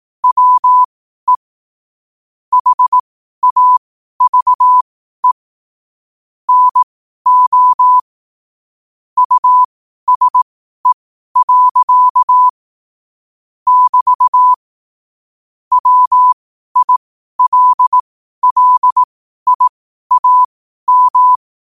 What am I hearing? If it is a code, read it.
Morse code, heard as a keyed tone, about 18 wpm, spaced as for 12 wpm, 1000 Hz. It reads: WE HAVE NO USE. <BT> WILLIAM
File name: qotdfiles / 12wpm